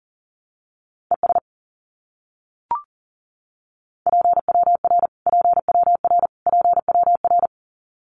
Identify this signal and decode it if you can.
Morse code, heard as a keyed tone, about 37 wpm, 700 Hz. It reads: ES PWR PWR PWR
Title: Abbreviations Repeated 3x - 40wpm